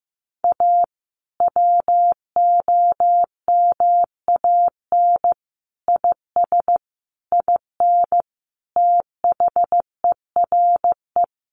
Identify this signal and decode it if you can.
Morse code, heard as a keyed tone, about 15 wpm, 700 Hz. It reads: A WOMAN IS IN THERE
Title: Sentences from Top 1400 Words 15wpm